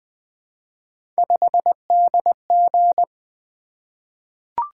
Morse code, heard as a keyed tone, about 20 wpm, 700 Hz. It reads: 5DG
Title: Three Letter-Number Combinations - 20wpm